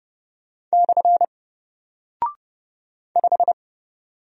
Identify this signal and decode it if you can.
Morse code, heard as a keyed tone, about 30 wpm, 700 Hz. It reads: / 5
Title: Single Character (Letter:Number:Punctuation:Prosign) Alt 30wpm